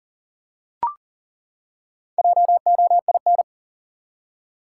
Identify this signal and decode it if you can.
Morse code, heard as a keyed tone, about 40 wpm, 700 Hz. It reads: JOIN
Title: Four Letter Words - 40wpm